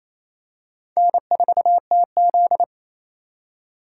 Morse code, heard as a keyed tone, about 28 wpm, 700 Hz. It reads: N4TZ
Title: K1USN SST QSO Practice - 28wpm